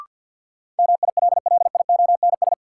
Morse code, heard as a keyed tone, about 53 wpm, 700 Hz. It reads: MILLIONS
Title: Sets of 5 Words - New Word Drill - 50wpm